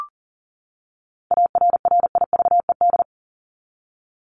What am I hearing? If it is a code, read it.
Morse code, heard as a keyed tone, about 38 wpm, 700 Hz. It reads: ARRIVED
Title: ED Suffix - Top 100 Words with Base Words 40wpm